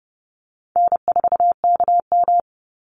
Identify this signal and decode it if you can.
Morse code, heard as a keyed tone, about 30 wpm, 700 Hz. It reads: N4KM
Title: Easy Call Signs - 30wpm